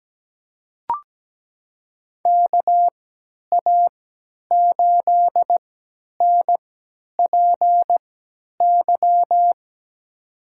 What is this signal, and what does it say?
Morse code, heard as a keyed tone, about 17 wpm, spaced as for 10 wpm, 700 Hz. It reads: KA8NPY
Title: USA Call Signs - 6 Characters - 3x Character Spacing - 17wpm